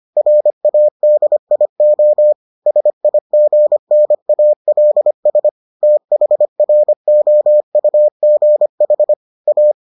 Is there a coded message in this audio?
Yes: Morse code from 0.2 to 9.7 s, heard as a keyed tone, about 25 wpm, 600 Hz. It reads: RADIO SIGNALS THROUGH A